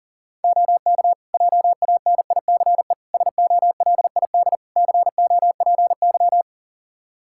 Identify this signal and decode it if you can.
Morse code, heard as a keyed tone, about 37 wpm, 700 Hz. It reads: OK JANICE SOLID COPY
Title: K1USN SST QSO Practice 40wpm